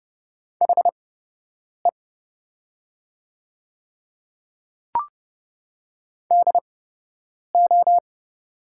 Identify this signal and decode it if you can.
Morse code, heard as a keyed tone, about 31 wpm, spaced as for 8 wpm, 700 Hz. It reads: HE DO
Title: 8x Character Spacing - Top 100 Words - 30wpm